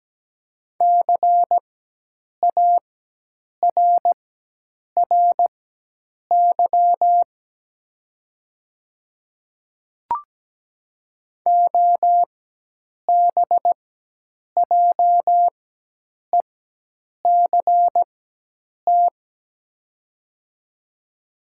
Morse code, heard as a keyed tone, about 17 wpm, spaced as for 8 wpm, 700 Hz. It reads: CARRY OBJECT
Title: 4x Character Spacing - Top 500 Words - 17wpm